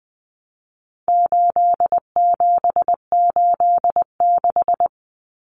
Morse code, heard as a keyed tone, about 20 wpm, 700 Hz. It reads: 8786
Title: Numbers - 4-Digit 20wpm